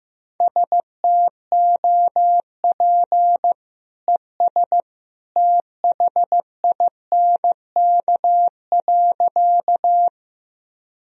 Morse code, heard as a keyed tone, about 15 wpm, 700 Hz. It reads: STOP ES THINK.
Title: Sentences from Top 300 Words and Abbreviations 15wpm